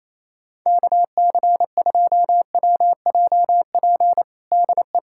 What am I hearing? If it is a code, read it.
Morse code, heard as a keyed tone, about 28 wpm, 700 Hz. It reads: KC2WJP DE
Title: USA Callsign DE Callsign - 6 Characters - 28wpm